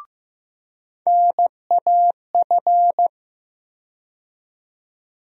Morse code, heard as a keyed tone, about 15 wpm, 700 Hz. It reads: NAF
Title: USA Call Sign Postfixes - 3 Letters 15wpm